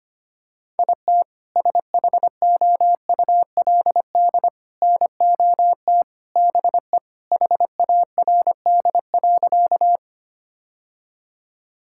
Morse code, heard as a keyed tone, about 25 wpm, 700 Hz. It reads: IT SHOULD NOT BE HARD.
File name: practice / 25wpm / Sentences with 2-Word Phrases - 25wpm